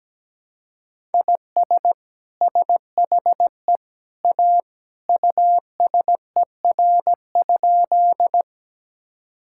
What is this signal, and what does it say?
Morse code, heard as a keyed tone, about 17 wpm, 700 Hz. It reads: IS SHE A USER?
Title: Sentences from Top 1000 Words - Review Part 5 - 17wpm